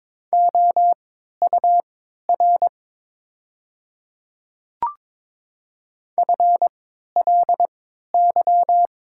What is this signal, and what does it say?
Morse code, heard as a keyed tone, about 22 wpm, spaced as for 12 wpm, 700 Hz. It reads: OUR FLY
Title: 3x Character Spacing - Top 500 Words - 22wpm